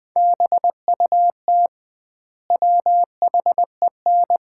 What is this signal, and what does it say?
Morse code, heard as a keyed tone, about 20 wpm, 700 Hz. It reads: BUT WHEN
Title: 2x Word-Spacing - Sets of 5 Words - 20wpm